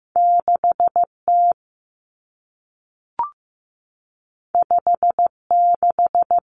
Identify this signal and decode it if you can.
Morse code, heard as a keyed tone, about 15 wpm, 700 Hz. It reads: 6T 56
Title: Common Cut Numbers - 2-Digit 15wpm